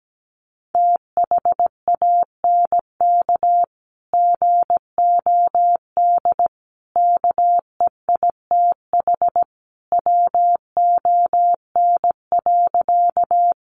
Morse code, heard as a keyed tone, about 17 wpm, 700 Hz. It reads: THANK GOD KEITH WON.
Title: A Sound Of Thunder 17wpm